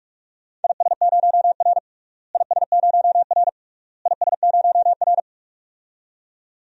Morse code, heard as a keyed tone, about 46 wpm, 700 Hz. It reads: IS0R IS0R IS0R
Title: DXCC-100 Call Signs - 4 Characters - Repeated 3x - 45wpm